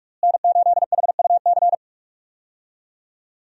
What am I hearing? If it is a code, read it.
Morse code, heard as a keyed tone, about 46 wpm, 700 Hz. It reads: N8HUC
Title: USA Call Signs - 5 Characters - Repeated 3x - 45wpm